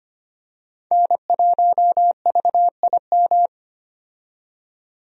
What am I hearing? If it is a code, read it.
Morse code, heard as a keyed tone, about 25 wpm, 690 Hz. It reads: N1VIM
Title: USA Call Signs - 5 Characters - 25wpm